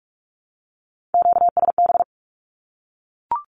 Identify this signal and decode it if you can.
Morse code, heard as a keyed tone, about 44 wpm, 700 Hz. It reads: QSB
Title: Q-Codes 45wpm